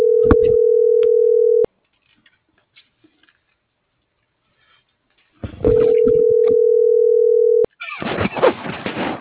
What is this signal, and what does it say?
Ringback tone (North America), 440 + 480 Hz.